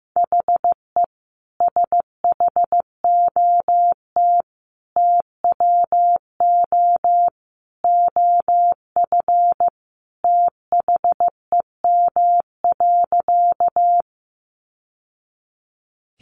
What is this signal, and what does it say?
Morse code, heard as a keyed tone, about 15 wpm, 700 Hz. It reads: HE SHOT TWO OF THEM.
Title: Sentences from Top 1100 Words 15wpm